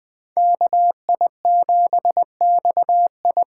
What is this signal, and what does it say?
Morse code, heard as a keyed tone, about 20 wpm, 700 Hz. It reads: KI7XI